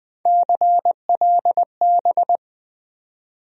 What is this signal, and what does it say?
Morse code, heard as a keyed tone, about 20 wpm, 700 Hz. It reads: CLB
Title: USA Call Sign Postfixes - 3 Letters 20wpm